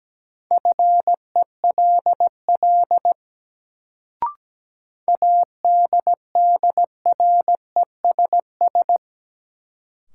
Morse code, heard as a keyed tone, about 17 wpm, 700 Hz. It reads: FELL ADDRESS
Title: Sets of 5 Words Encore - New Word Drill - 17wpm